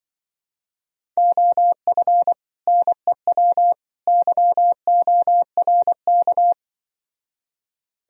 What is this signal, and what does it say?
Morse code, heard as a keyed tone, about 24 wpm, 700 Hz. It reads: OF NEW YORK
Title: Speed-Racing - Sets of 3 Words 20wpm